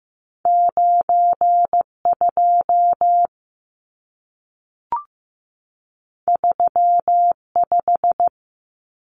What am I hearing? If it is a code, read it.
Morse code, heard as a keyed tone, about 15 wpm, 700 Hz. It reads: 92 35
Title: Numbers - 2-Digit 15wpm